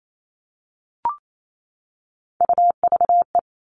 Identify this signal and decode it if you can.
Morse code, heard as a keyed tone, about 28 wpm, 700 Hz. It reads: UVE
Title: All Cut Numbers - 3-Digit 28wpm